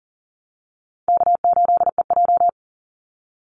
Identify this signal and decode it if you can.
Morse code, heard as a keyed tone, about 42 wpm, 700 Hz. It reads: K8EJ
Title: Easy Call Signs - 40wpm